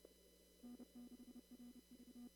Morse code, heard as a keyed tone, about 30 wpm, 240 Hz. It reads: N6RV
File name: CW / kt5j wpx cw-op-k5pi